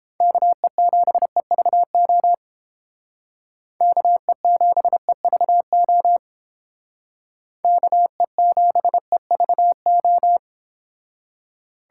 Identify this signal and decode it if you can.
Morse code, heard as a keyed tone, about 28 wpm, 700 Hz. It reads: KE7EVO KE7EVO KE7EVO